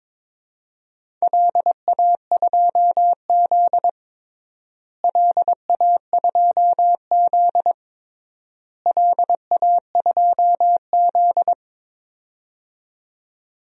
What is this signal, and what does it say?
Morse code, heard as a keyed tone, about 21 wpm, 700 Hz. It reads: LA2Z LA2Z LA2Z
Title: DXCC-100 Call Signs - 4 Characters - Repeated 3x - 22wpm